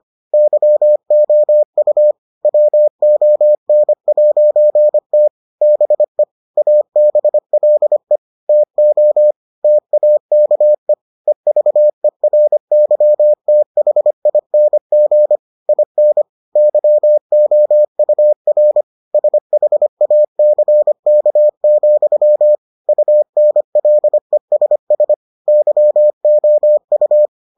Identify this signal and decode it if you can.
Morse code, heard as a keyed tone, about 25 wpm, 600 Hz. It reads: YOU WON'T BE ABLE TO TAKE EVERYTHING IN YOUR SHACK, UNLESS YOU